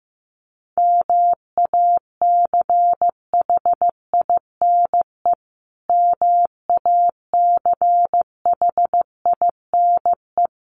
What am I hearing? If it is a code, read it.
Morse code, heard as a keyed tone, about 14 wpm, 700 Hz. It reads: MACHINE MACHINE
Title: Top 500 Words Repeated 15wpm